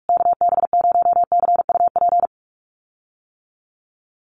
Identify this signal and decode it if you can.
Morse code, heard as a keyed tone, about 44 wpm, 700 Hz. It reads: KB0CUP
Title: USA Call Signs Repeated 3x 45wpm